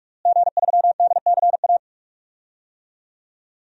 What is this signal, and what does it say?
Morse code, heard as a keyed tone, about 45 wpm, 700 Hz. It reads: G3DCA
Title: DXCC-100 Call Signs - 5 Characters - Repeated 3x - 45wpm